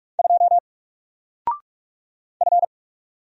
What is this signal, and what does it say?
Morse code, heard as a keyed tone, about 44 wpm, 700 Hz. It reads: 2 F
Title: Single Letter-Number Alt 45wpm